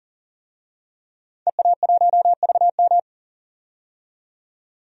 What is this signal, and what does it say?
Morse code, heard as a keyed tone, about 40 wpm, 700 Hz. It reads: EA1VM
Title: DXCC-100 Call Signs - 5 Characters - 40wpm